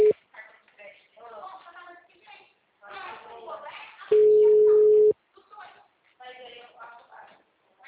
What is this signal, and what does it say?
Ringback tone (Europe, 425 Hz), with speech.
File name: out- -2003- - - .3223